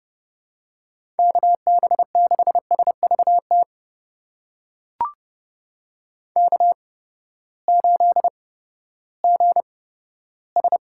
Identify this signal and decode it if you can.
Morse code, heard as a keyed tone, about 30 wpm, spaced as for 15 wpm, 700 Hz. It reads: KB6SVT K8GS
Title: USA Call Signs - 8x Character Spacing - 30wpm